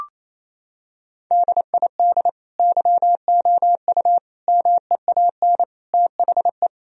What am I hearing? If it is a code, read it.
Morse code, heard as a keyed tone, about 28 wpm, 700 Hz. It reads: DID YOU MEAN THE
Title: Sentences from Top 1000 Words - Review Part 2 - 28wpm